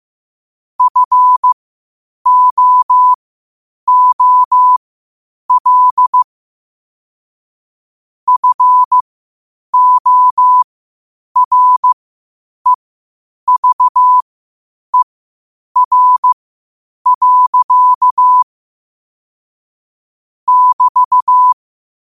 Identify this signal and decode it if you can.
Morse code, heard as a keyed tone, about 15 wpm, spaced as for 8 wpm, 1000 Hz. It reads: FOOL FOREVER. <BT>
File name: qotdfiles / 8wpm